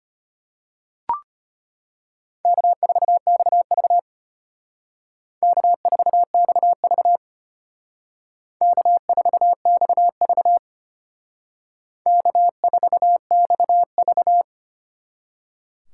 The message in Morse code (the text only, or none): K4XV K4XV K4XV K4XV